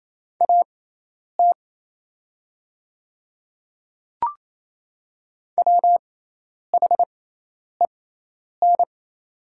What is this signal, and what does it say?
Morse code, heard as a keyed tone, about 27 wpm, spaced as for 10 wpm, 690 Hz. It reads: AT WHEN
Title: 6x Character Spacing - Top 100 Words - 28wpm